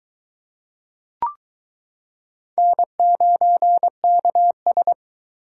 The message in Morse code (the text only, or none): N9KS